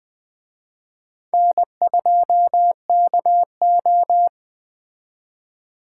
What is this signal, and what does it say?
Morse code, heard as a keyed tone, about 20 wpm, 700 Hz. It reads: N2KO